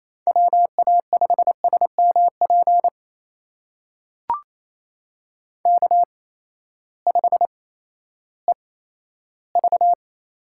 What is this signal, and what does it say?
Morse code, heard as a keyed tone, about 27 wpm, spaced as for 14 wpm, 700 Hz. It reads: WA5SMP K5EV
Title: USA Call Signs - 8x Character Spacing - 28wpm